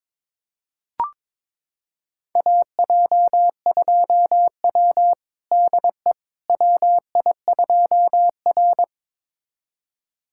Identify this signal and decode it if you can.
Morse code, heard as a keyed tone, about 22 wpm, 700 Hz. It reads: AJ2W DE WI2R
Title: USA Callsign DE Callsign - 4 Characters - 22wpm